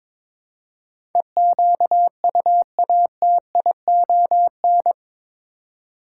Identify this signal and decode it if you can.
Morse code, heard as a keyed tone, about 22 wpm, 700 Hz. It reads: EQUATION
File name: practice / 22wpm / Eight Letter Words - 22wpm